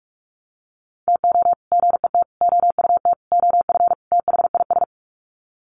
Morse code, heard as a keyed tone, about 44 wpm, 700 Hz. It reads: TO GET OUT OF THIS
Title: Sets of 5 Words Encore2 45wpm